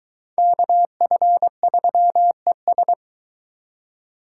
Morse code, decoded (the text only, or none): KF3ES